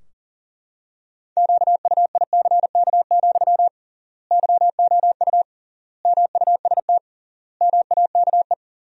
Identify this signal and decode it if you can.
Morse code, heard as a keyed tone, about 37 wpm, 700 Hz. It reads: QUICK, YOU MUST MAKE
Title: 3x Word-Spacing - Sentences from Top 500 Words - 40wpm